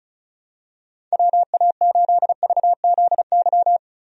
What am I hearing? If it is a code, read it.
Morse code, heard as a keyed tone, about 36 wpm, 700 Hz. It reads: WA8VZY
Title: USA Call Signs - 6 Characters - Repeated 3x - 35wpm